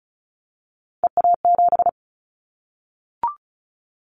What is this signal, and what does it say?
Morse code, heard as a keyed tone, about 34 wpm, 700 Hz. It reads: EA7
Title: Callsign Prefixes for Top 25 DXCC - Repeated 3x - 35wpm